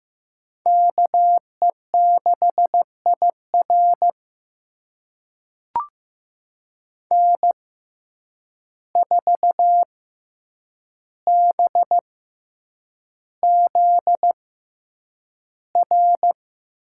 Morse code, heard as a keyed tone, about 15 wpm, spaced as for 8 wpm, 700 Hz. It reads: KE6IR N4BZR
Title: USA Call Signs - 5 Characters - 6x Character Spacing - 15wpm